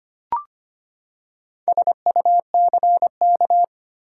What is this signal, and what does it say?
Morse code, heard as a keyed tone, about 25 wpm, 700 Hz. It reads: SUCK
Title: Four Letter Words - 25wpm